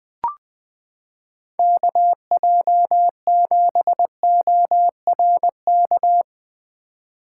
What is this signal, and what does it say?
Morse code, heard as a keyed tone, about 20 wpm, 700 Hz. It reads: KJ7ORK
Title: 2023 POTA Activator Call Signs - 20wpm